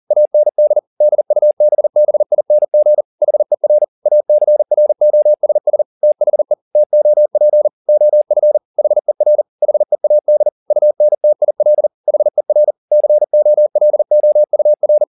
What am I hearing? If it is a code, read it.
Morse code, heard as a keyed tone, about 40 wpm, 600 Hz. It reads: AND DUBBING HER ACROSS THE TOP OF HER HEAD UNTIL HER COLOUR